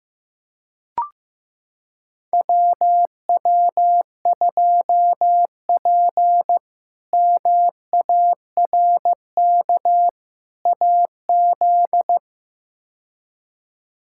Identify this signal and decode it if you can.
Morse code, heard as a keyed tone, about 15 wpm, 700 Hz. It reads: WW2P MARK AZ
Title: K1USN SST QSO Practice - 15wpm